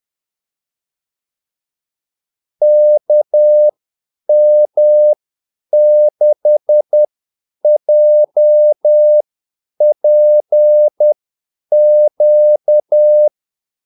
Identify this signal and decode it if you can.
Morse code, heard as a keyed tone, about 10 wpm, spaced as for 8 wpm, 600 Hz. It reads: KM6JPQ
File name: Callsigns-10-8wpm